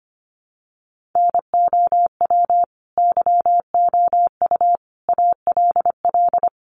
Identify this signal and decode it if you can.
Morse code, heard as a keyed tone, about 25 wpm, 700 Hz. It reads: NOW YOU ALL
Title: Sentences from Top 1200 Words 25wpm